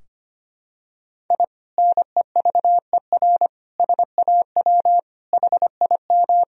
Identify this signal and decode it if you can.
Morse code, heard as a keyed tone, about 25 wpm, 700 Hz. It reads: I NEVER SAW HIM
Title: Sentences from Top 1000 Words - Review Part 2 - 25wpm